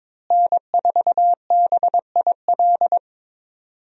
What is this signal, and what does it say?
Morse code, heard as a keyed tone, about 21 wpm, 700 Hz. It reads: N4BIL
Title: USA Call Signs - 5 Characters - Repeated 2x - 22wpm